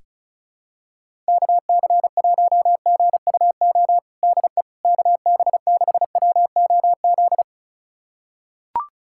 Morse code, heard as a keyed tone, about 36 wpm, 700 Hz. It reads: KC1GUO DE KB6WOZ
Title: USA Callsign DE Callsign 35wpm